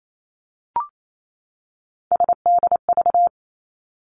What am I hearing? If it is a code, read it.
Morse code, heard as a keyed tone, about 28 wpm, 700 Hz. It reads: SDV
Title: Three Challenging Letter-Number Combinations 28wpm